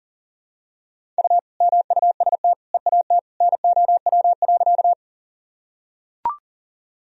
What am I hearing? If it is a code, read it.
Morse code, heard as a keyed tone, about 40 wpm, 700 Hz. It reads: U MUST EAT NOW.
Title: Sentences from Top 300 Words and Abbreviations 40wpm